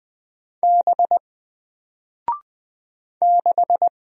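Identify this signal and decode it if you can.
Morse code, heard as a keyed tone, about 20 wpm, 700 Hz. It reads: B 6
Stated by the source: Single letter and number Morse drill at 20 wpm, from N4LCD.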